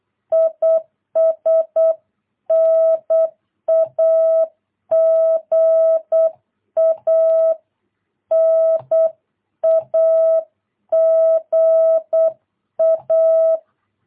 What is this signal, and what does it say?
Morse code, heard as a keyed tone, about 8 wpm, 660 Hz. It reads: IS NAGA NAGA
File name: jh8cbh / 2022cw